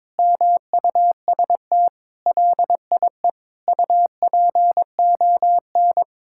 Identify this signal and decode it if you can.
Morse code, heard as a keyed tone, about 22 wpm, 700 Hz. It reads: MUST LIE UPON
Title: Sentences from Top 1600 Words 22wpm